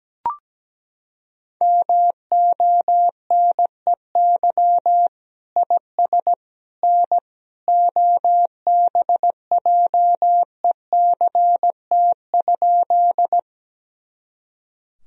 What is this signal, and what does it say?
Morse code, heard as a keyed tone, about 17 wpm, 700 Hz. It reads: MONEY IS N OBJECT?
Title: Sentences from Top 500 Words and Abbreviations 17wpm